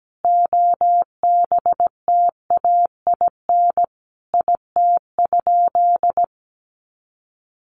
Morse code, heard as a keyed tone, about 17 wpm, 700 Hz. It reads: OBTAIN IT?
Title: Sentences from Top 1300 Words 17wpm